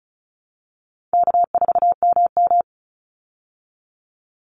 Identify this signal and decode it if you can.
Morse code, heard as a keyed tone, about 34 wpm, 700 Hz. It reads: K4MM
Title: Easy Call Signs Repeated 3x - 35wpm